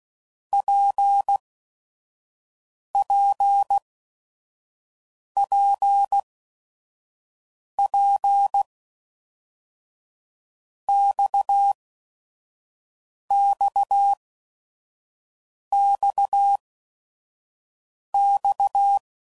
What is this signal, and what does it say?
Morse code, heard as a keyed tone, about 16 wpm, spaced as for 5 wpm, 800 Hz. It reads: PPPP XXXX
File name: Basic Letter Groups By7Groups / RKPX / RKPX@5 f15